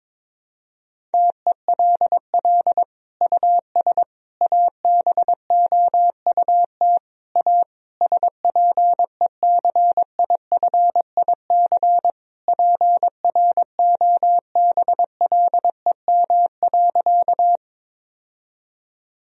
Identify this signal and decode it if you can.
Morse code, heard as a keyed tone, about 21 wpm, 700 Hz. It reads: TELL US ABOUT A SPECIFIC PROBLEM.
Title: Sentences from Top 1000 Words - Review Part 3 - 22wpm